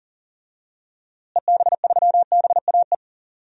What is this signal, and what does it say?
Morse code, heard as a keyed tone, about 40 wpm, 700 Hz. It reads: EB3BAE
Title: DXCC-100 Call Signs - 6 Characters - Repeated 3x - 40wpm